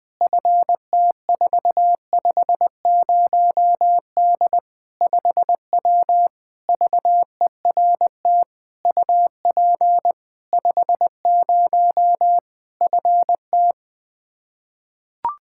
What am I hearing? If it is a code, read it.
Morse code, heard as a keyed tone, about 20 wpm, 700 Hz. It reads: FT450D 5W VERT UP 50 FT
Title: Practice QSO - Exchange 2 - Info - 20wpm